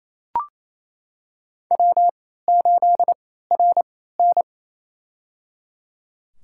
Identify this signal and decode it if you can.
Morse code, heard as a keyed tone, about 28 wpm, spaced as for 16 wpm, 700 Hz. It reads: W8RN
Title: USA Call Signs - 4 Characters - 3x Character Spacing - 28wpm